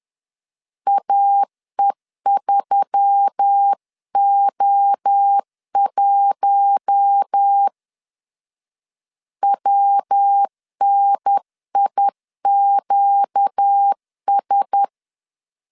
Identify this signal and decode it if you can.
Morse code, heard as a keyed tone, about 10 wpm, 800 Hz. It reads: AE3O1 WNIQS
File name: CW / DET / 1 6 1500C S B 10wpm 4137